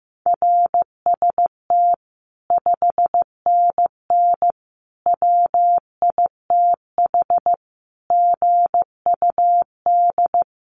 Morse code, heard as a keyed tone, about 15 wpm, 700 Hz. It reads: RST 5NN WITH GUD
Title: Common QSO Phrases 15wpm